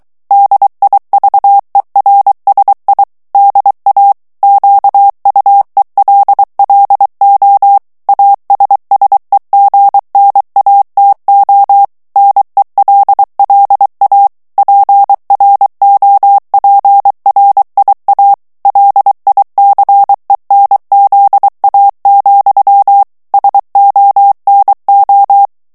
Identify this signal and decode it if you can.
Morse code, heard as a keyed tone, about 23 wpm, 800 Hz. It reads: DIVERSI DA QUELLO ASSEGNATO NELLA PROPRIA LICENZA, SONO
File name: ITALIEN texte clair 2 vitesse 22 mots minute